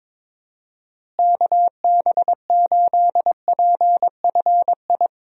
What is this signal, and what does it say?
Morse code, heard as a keyed tone, about 22 wpm, 700 Hz. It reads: KB8PFI
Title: USA Call Signs - 6 Characters - Repeated 2x - 22wpm